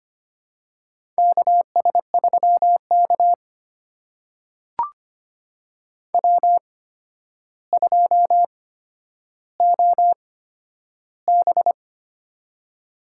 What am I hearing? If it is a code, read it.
Morse code, heard as a keyed tone, about 25 wpm, spaced as for 11 wpm, 700 Hz. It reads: KS3K W2OB
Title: USA Call Signs - 4 Characters - 8x Character Spacing - 25wpm